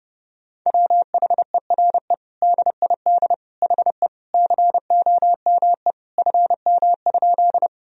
Morse code, heard as a keyed tone, about 30 wpm, 700 Hz. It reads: WHERE DID HE COME FM?
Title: Sentences from Top 200 Words and Abbreviations 30wpm